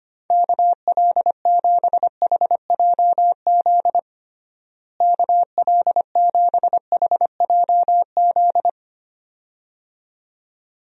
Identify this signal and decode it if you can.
Morse code, heard as a keyed tone, about 25 wpm, 700 Hz. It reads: KL7HJZ KL7HJZ
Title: DXCC-100 Call Signs - 6 Characters - Repeated 2x - 25wpm